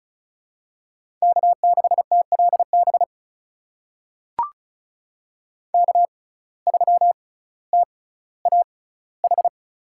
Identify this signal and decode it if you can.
Morse code, heard as a keyed tone, about 36 wpm, spaced as for 18 wpm, 700 Hz. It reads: K6TLB K3TAH
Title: USA Call Signs - 5 Characters - 6x Character Spacing - 35wpm